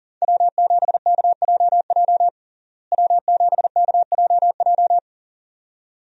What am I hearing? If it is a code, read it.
Morse code, heard as a keyed tone, about 40 wpm, 700 Hz. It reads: W7KJJ W7KJJ
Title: USA Call Signs - 5 Characters - Repeated 2x - 40wpm